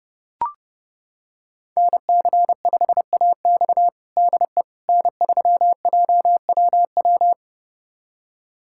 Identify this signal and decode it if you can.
Morse code, heard as a keyed tone, about 30 wpm, 700 Hz. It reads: NC5AX DE N3JWW
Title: USA Callsign DE Callsign - 5 Characters - 30wpm